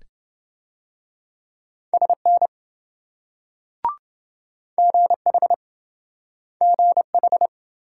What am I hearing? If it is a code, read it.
Morse code, heard as a keyed tone, about 27 wpm, 700 Hz. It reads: SN GH GH